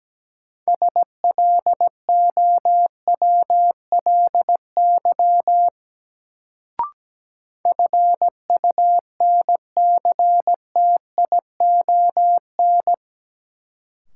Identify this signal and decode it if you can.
Morse code, heard as a keyed tone, about 17 wpm, 700 Hz. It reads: SLOWLY FUNCTION
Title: Sentences from Top 1100 Words - New Word Drill - 17wpm